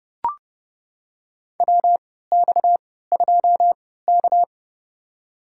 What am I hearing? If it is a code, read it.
Morse code, heard as a keyed tone, about 31 wpm, spaced as for 17 wpm, 700 Hz. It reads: WX2K
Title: USA Call Signs - 3x Character Spacing - 30wpm